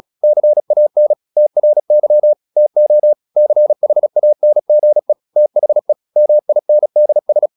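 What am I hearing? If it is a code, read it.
Morse code, heard as a keyed tone, about 36 wpm, 600 Hz. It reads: CAN TRY TO CHANGE THE MINDS
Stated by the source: Morse practice at 36 wpm, text of Ken Follett's novel Never, from CW med Gnister.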